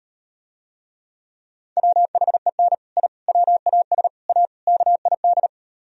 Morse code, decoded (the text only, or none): WHEN I WAS A KID